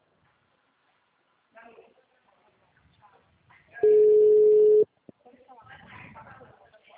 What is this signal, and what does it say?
Ringback tone (Europe, 425 Hz), with speech.
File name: out- -2011- - - .895